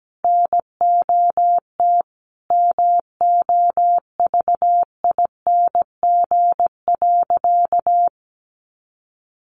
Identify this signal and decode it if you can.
Morse code, heard as a keyed tone, about 17 wpm, 700 Hz. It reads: NOT MOVING.
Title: A Sound Of Thunder 17wpm